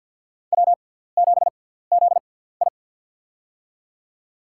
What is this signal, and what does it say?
Morse code, heard as a keyed tone, about 51 wpm, spaced as for 17 wpm, 700 Hz. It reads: W7ZI
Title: USA Call Signs - 4 Characters - 6x Character Spacing - 50wpm